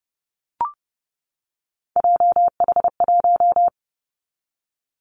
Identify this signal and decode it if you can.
Morse code, heard as a keyed tone, about 30 wpm, 700 Hz. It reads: JH1
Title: Callsign Prefixes for Top 25 DXCC - 30wpm